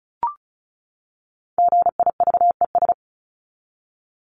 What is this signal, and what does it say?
Morse code, heard as a keyed tone, about 34 wpm, 700 Hz. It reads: GIVES
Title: ES:S Suffix - Top 100 Words with Base Words 35wpm